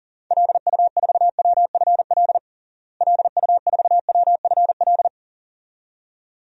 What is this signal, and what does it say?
Morse code, heard as a keyed tone, about 40 wpm, 700 Hz. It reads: LU4WFL LU4WFL
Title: DXCC-100 Call Signs - 6 Characters - Repeated 2x - 40wpm